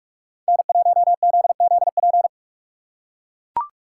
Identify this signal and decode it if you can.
Morse code, heard as a keyed tone, about 43 wpm, 700 Hz. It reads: N1ZZP